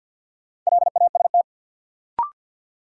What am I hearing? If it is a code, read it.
Morse code, heard as a keyed tone, about 53 wpm, 700 Hz. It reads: LAST
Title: Sentences from Top 300 Words - New Word Drill - 50wpm